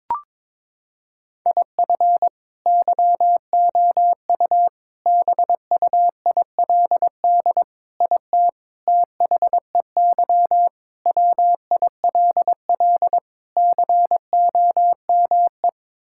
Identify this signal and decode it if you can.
Morse code, heard as a keyed tone, about 22 wpm, 700 Hz. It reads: IF YOU BUILD IT THEY WILL COME
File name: practice / 22wpm / Sets of 7 Words - 22wpm